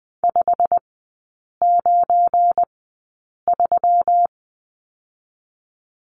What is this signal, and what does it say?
Morse code, heard as a keyed tone, about 21 wpm, 700 Hz. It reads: 5 9 3
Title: 172 Lesson 28e - Number 9 - Three Characters 20wpm